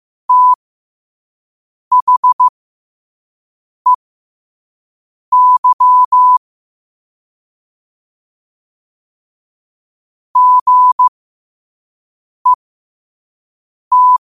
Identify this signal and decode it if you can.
Morse code, heard as a keyed tone, about 15 wpm, spaced as for 5 wpm, 1000 Hz. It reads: THEY GET
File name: qotdfiles / 5wpm